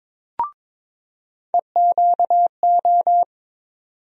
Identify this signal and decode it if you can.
Morse code, heard as a keyed tone, about 21 wpm, 700 Hz. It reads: EQO
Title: USA Call Sign Postfixes - 3 Letters 22wpm